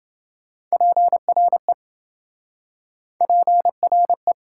Morse code, heard as a keyed tone, about 27 wpm, 700 Hz. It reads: PRE PRE